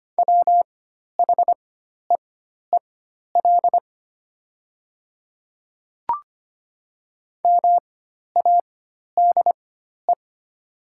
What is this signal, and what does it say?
Morse code, heard as a keyed tone, about 25 wpm, spaced as for 12 wpm, 700 Hz. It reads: WHEEL MADE
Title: 4x Character Spacing - Top 500 Words - 25wpm